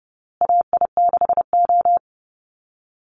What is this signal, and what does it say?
Morse code, heard as a keyed tone, about 30 wpm, 700 Hz. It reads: AI6O
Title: Call Signs Repeated 2x - 30wpm